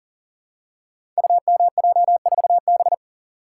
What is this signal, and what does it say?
Morse code, heard as a keyed tone, about 42 wpm, 700 Hz. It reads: UMJ4B
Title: Code Groups - 5 Letter:Numbers - 40wpm